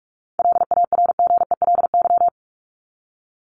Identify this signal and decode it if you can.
Morse code, heard as a keyed tone, about 44 wpm, 700 Hz. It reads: LARGELY
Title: LY Suffix - Top 100 Words with Base Words 45wpm